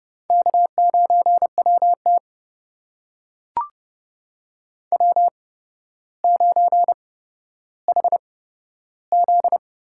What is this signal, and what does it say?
Morse code, heard as a keyed tone, about 30 wpm, spaced as for 13 wpm, 700 Hz. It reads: K9WT W9HZ
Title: USA Call Signs - 4 Characters - 8x Character Spacing - 30wpm